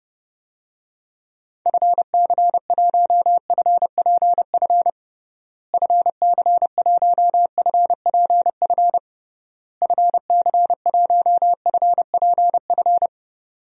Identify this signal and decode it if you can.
Morse code, heard as a keyed tone, about 32 wpm, 700 Hz. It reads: FC1FPF FC1FPF FC1FPF
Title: DXCC-100 Call Signs - 6 Characters - Repeated 3x - 30wpm